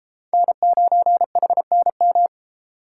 Morse code, heard as a keyed tone, about 31 wpm, 700 Hz. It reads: N9HNM